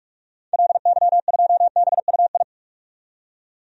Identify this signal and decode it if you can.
Morse code, heard as a keyed tone, about 44 wpm, 700 Hz. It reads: LY2BUI
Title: DXCC-100 Call Signs - 6 Characters - 45wpm